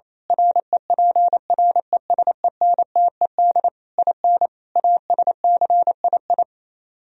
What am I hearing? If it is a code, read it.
Morse code, heard as a keyed tone, about 27 wpm, 700 Hz. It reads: REPRESENTED IN ASCII
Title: Speed-Racing - Sets of 5 Words 28wpm